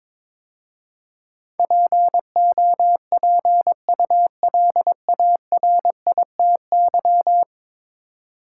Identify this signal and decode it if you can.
Morse code, heard as a keyed tone, about 22 wpm, 700 Hz. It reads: POPULARITY
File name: practice / 22wpm / 10 Letter Words - First 3 Letters Given - 22wpm